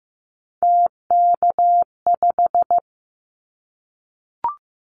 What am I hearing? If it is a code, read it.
Morse code, heard as a keyed tone, about 15 wpm, 700 Hz. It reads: TK5
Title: Callsign Prefixes for Top 100 DXCC - 15wpm